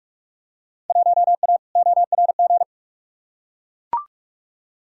Morse code, heard as a keyed tone, about 42 wpm, 700 Hz. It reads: 1A ORG
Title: Field Day Exchanges - 45wpm